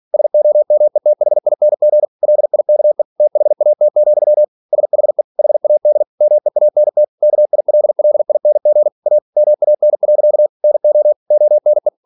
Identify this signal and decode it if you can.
Morse code, heard as a keyed tone, about 46 wpm, 600 Hz. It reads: SOMETHING LIKE THAT, SHE HAD MEANT KILLING A MAN. NO ONE